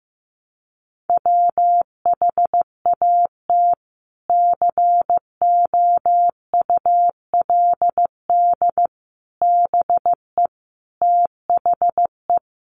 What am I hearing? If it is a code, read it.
Morse code, heard as a keyed tone, about 15 wpm, 700 Hz. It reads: WHAT COULD BE THE
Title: Sentences from Top 900 Words 15wpm